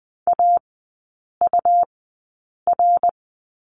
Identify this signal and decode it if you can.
Morse code, heard as a keyed tone, about 20 wpm, spaced as for 8 wpm, 700 Hz. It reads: AUR
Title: 121 Lesson 20e - Number 3 - Three Characters 20wpm